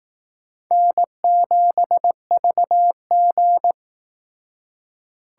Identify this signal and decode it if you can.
Morse code, heard as a keyed tone, about 18 wpm, 700 Hz. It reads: N7VG
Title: Speed Racing - USA 1x2 Call Signs 15wpm